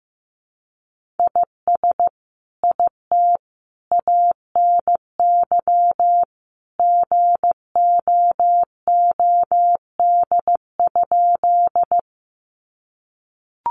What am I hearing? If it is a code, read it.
Morse code, heard as a keyed tone, about 14 wpm, 700 Hz. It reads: IS IT ANY GOOD?
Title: Sentences from Top 200 Words 15wpm